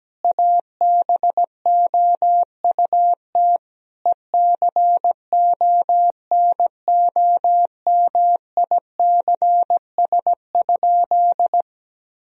Morse code, heard as a keyed tone, about 17 wpm, 700 Hz. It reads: ABOUT ECONOMICS?
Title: Sentences from Top 1000 Words - Review Part 3 - 17wpm